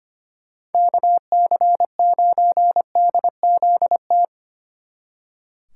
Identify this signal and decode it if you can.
Morse code, heard as a keyed tone, about 32 wpm, 700 Hz. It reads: KC9DZT